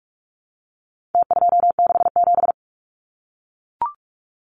Morse code, heard as a keyed tone, about 44 wpm, 700 Hz. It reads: T267
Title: Numbers - Contest Style - 4-Digit 45wpm